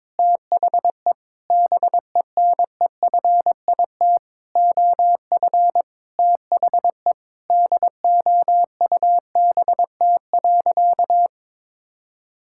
Morse code, heard as a keyed tone, about 22 wpm, 700 Hz. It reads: THE BENEFIT OF THE DOUBT.
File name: Sentences with 7-Word Phrases - 22wpm